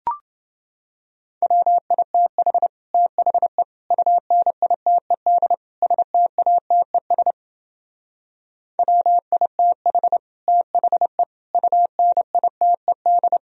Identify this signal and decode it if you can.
Morse code, heard as a keyed tone, about 29 wpm, 700 Hz. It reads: WITH THE UNITED STATES WITH THE UNITED